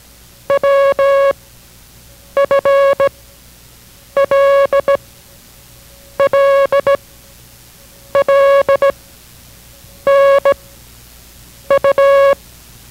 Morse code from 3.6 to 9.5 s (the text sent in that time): L L L